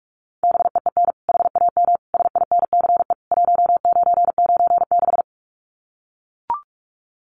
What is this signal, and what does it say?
Morse code, heard as a keyed tone, about 44 wpm, 700 Hz. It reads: BEEN HAM SINCE 1996
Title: Common QSO Phrases 45wpm